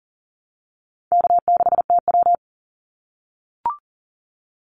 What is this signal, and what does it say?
Morse code, heard as a keyed tone, about 40 wpm, 700 Hz. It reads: K6TW
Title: NA SOTA Activator Callsigns 40wpm